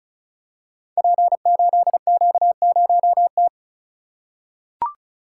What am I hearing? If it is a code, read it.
Morse code, heard as a keyed tone, about 36 wpm, 700 Hz. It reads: P8Q0T
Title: Code Groups - 5 Letter:Numbers - 35wpm